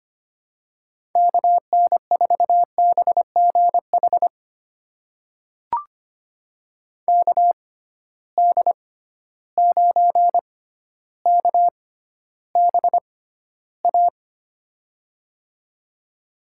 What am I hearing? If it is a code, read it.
Morse code, heard as a keyed tone, about 25 wpm, spaced as for 13 wpm, 700 Hz. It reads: KN4BGH KD9KBA
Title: USA Call Signs - 6 Characters - 6x Character Spacing - 25wpm